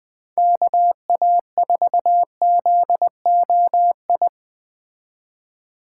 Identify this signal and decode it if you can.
Morse code, heard as a keyed tone, about 21 wpm, 700 Hz. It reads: KA4ZOI
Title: Speed Racing - USA 2x3 Call Signs 17wpm